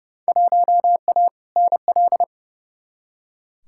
Morse code, heard as a keyed tone, about 32 wpm, 700 Hz. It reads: 1A NL
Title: Field Day Exchanges Encore - 30wpm